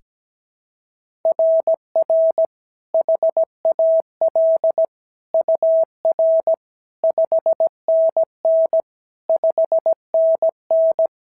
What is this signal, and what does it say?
Morse code, heard as a keyed tone, about 16 wpm, 650 Hz. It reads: RR HAL UR 5NN 5NN
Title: SOTA QSO Practice 17wpm